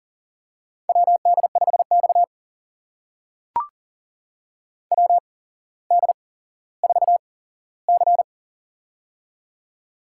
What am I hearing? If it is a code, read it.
Morse code, heard as a keyed tone, about 38 wpm, spaced as for 17 wpm, 700 Hz. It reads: WD5X WD4C
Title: USA Call Signs - 4 Characters - 8x Character Spacing - 40wpm